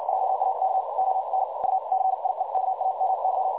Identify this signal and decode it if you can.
Morse code, heard as a keyed tone, about 56 wpm, spaced as for 46 wpm, 760 Hz, with noise. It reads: JAK ON UCIEKAMY
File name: dlazaawansowanych / nagraniazpasma / websdr recording 3565.0kHz